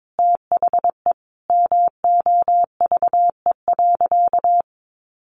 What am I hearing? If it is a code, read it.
Morse code, heard as a keyed tone, about 22 wpm, 700 Hz. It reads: THE MOVE.
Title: Sentences from Top 1400 Words 22wpm